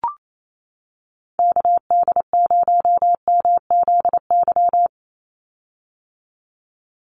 Morse code, heard as a keyed tone, about 28 wpm, 700 Hz. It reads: KD0MZY
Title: USA Call Signs 28wpm